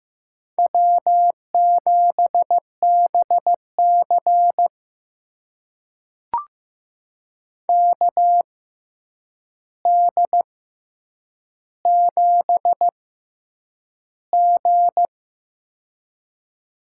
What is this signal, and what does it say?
Morse code, heard as a keyed tone, about 15 wpm, spaced as for 8 wpm, 700 Hz. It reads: W7BC KD7G
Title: USA Call Signs - 4 Characters - 6x Character Spacing - 15wpm